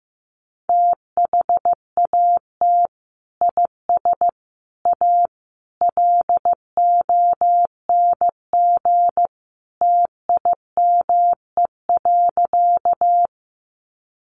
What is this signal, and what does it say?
Morse code, heard as a keyed tone, about 14 wpm, 700 Hz. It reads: THAT IS A LONG TIME.
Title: Sentences from Top 100 Words 15wpm